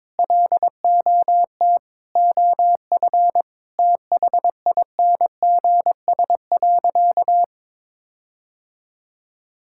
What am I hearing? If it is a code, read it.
Morse code, heard as a keyed tone, about 22 wpm, 700 Hz. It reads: LOT OF THINGS.
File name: practice / 22wpm / Sentences with 4-Word Phrases - 22wpm